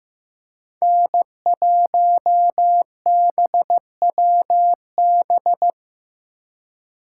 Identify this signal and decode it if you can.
Morse code, heard as a keyed tone, about 15 wpm, 700 Hz. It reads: N1BWB
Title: USA Call Signs - 5 Characters - 4x Character Spacing - 15wpm